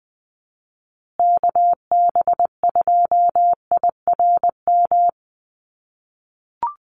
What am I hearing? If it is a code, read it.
Morse code, heard as a keyed tone, about 19 wpm, 700 Hz. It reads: KB2IRM
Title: USA Call Signs Repeated 2x 20wpm